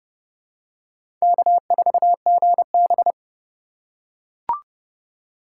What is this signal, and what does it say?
Morse code, heard as a keyed tone, about 32 wpm, 700 Hz. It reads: K4GB
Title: USA Call Signs - 4 Characters - 2x Character Spacing - 30wpm